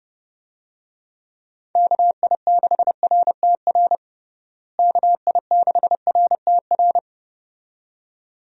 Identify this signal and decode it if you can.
Morse code, heard as a keyed tone, about 32 wpm, 700 Hz. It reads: KI6RTR KI6RTR
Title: USA Call Signs - 6 Characters - Repeated 2x - 30wpm